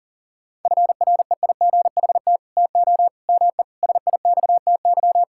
Morse code, heard as a keyed tone, about 40 wpm, 700 Hz. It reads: FREIGHT TO ME SIXTY
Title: Pangram Challenge v2 - 40wpm